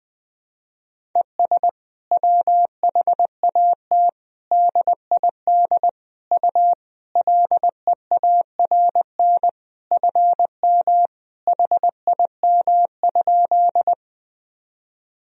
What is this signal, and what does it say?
Morse code, heard as a keyed tone, about 20 wpm, 700 Hz. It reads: ES WHAT DID U LEARN FM HIM?
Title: Sentences from Top 300 Words and Abbreviations 20wpm